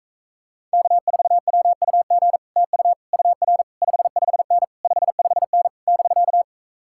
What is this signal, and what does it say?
Morse code, heard as a keyed tone, about 44 wpm, 700 Hz. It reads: K4WUG TU UR 55N 55N <BK>